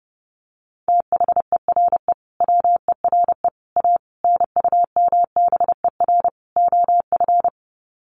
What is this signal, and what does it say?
Morse code, heard as a keyed tone, about 30 wpm, 700 Hz. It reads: THERE WERE A NUMBER OF
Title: Sets of 5 Words Encore2 30wpm